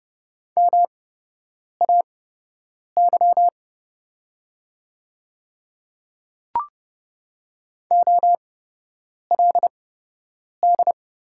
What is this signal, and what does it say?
Morse code, heard as a keyed tone, about 30 wpm, spaced as for 8 wpm, 700 Hz. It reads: MAY OLD
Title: 8x Character Spacing - Top 300 Words - 30wpm